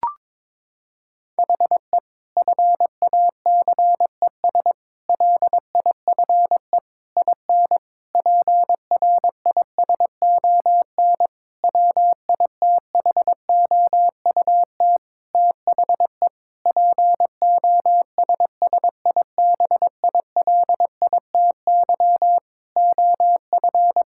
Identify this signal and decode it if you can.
Morse code, heard as a keyed tone, about 22 wpm, 700 Hz. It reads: HE FACES LIFE IN PRISON WITHOUT THE POSSIBILITY OF
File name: practice / 22wpm / Sentences with 8-Word Phrases - 22wpm